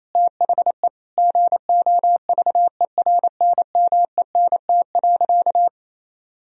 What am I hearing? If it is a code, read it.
Morse code, heard as a keyed tone, about 27 wpm, 700 Hz. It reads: THE GOVERNMENT.
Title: Sentences from Top 1000 Words - Review Part 3 - 28wpm